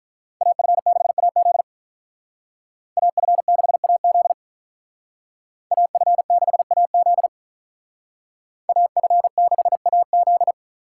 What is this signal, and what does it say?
Morse code, heard as a keyed tone, about 42 wpm, 700 Hz. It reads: AF6AZ AF6AZ AF6AZ AF6AZ
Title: Speed Racing - USA 2x2 Call Signs 35wpm